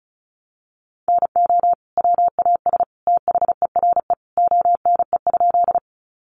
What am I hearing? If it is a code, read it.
Morse code, heard as a keyed tone, about 34 wpm, 700 Hz. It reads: NO WAS THERE ONE?
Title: Sentences from Top 100 Words 35wpm